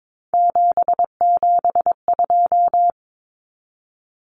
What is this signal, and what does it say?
Morse code, heard as a keyed tone, about 22 wpm, 700 Hz. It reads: 772
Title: Numbers - 3-Digit 22wpm